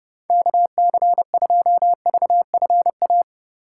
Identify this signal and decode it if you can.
Morse code, heard as a keyed tone, about 30 wpm, 700 Hz. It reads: KC2VFA
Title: USA Call Signs - 6 Characters - 2x Character Spacing - 30wpm